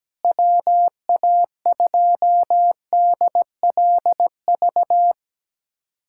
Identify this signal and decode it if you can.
Morse code, heard as a keyed tone, about 17 wpm, 700 Hz. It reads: WA2DLV
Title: USA Callsign DE Callsign - 6 Characters - 17wpm